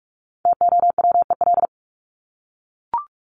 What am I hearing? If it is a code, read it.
Morse code, heard as a keyed tone, about 44 wpm, 700 Hz. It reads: TOWEL
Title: Top 1501-1750 Words 45wpm